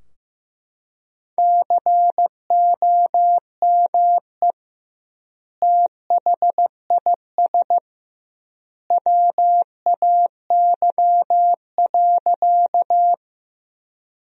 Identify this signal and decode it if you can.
Morse code, heard as a keyed tone, about 16 wpm, 700 Hz. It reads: COME THIS WAY.
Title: 2x Word-Spacing - Sentences from Top 100 Words - 15wpm